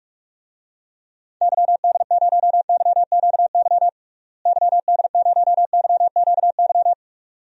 Morse code, heard as a keyed tone, about 44 wpm, 700 Hz. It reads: YD0YQY YD0YQY
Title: DXCC-100 Call Signs - 6 Characters - Repeated 2x - 45wpm